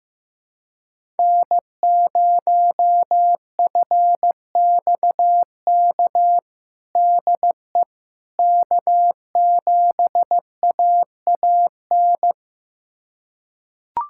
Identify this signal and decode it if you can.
Morse code, heard as a keyed tone, about 15 wpm, 700 Hz. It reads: N0FXK DE K7AAN
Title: USA Callsign DE Callsign - 5 Characters - 15wpm